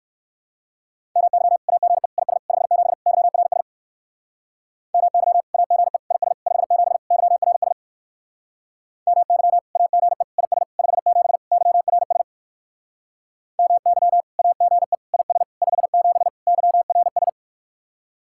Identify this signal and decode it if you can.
Morse code, heard as a keyed tone, about 56 wpm, 700 Hz. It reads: MY AGE IS 57 YRS MY AGE IS 57 YRS MY AGE IS 57 YRS MY AGE IS 57 YRS
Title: Speed-Racing Common QSO Phrases - 45wpm